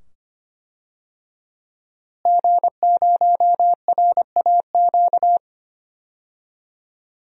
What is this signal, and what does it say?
Morse code, heard as a keyed tone, about 25 wpm, 700 Hz. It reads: G0RAQ
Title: DXCC-100 Call Signs - 5 Characters - Repeated 2x - 25wpm